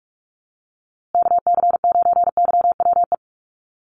Morse code, heard as a keyed tone, about 44 wpm, 700 Hz. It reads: KC9YWE
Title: USA Call Signs Repeated 3x 45wpm